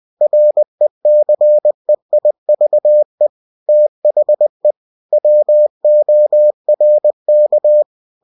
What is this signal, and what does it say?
Morse code, heard as a keyed tone, about 20 wpm, 600 Hz. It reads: RECEIVE THE WORK
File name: cw-books / winnie-the-pooh / winnie-20wpm-page-0119